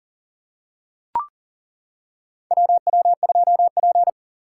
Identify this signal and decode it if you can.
Morse code, heard as a keyed tone, about 40 wpm, 700 Hz. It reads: WW2P
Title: K1USN SST QSO Practice - 40wpm